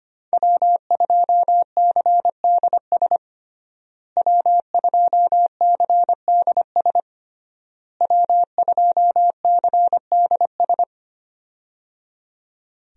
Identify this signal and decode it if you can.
Morse code, heard as a keyed tone, about 25 wpm, 700 Hz. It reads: W2CDS W2CDS W2CDS
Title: USA Call Signs - 5 Characters - Repeated 3x - 25wpm